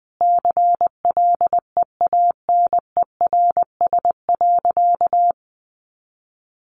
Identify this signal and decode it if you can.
Morse code, heard as a keyed tone, about 20 wpm, 700 Hz. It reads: CLEANERS.
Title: Additional Easy Phrases 20wpm